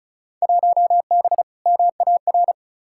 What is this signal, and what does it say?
Morse code, heard as a keyed tone, about 36 wpm, 700 Hz. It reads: 1B MAR
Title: Field Day Exchanges Encore - 35wpm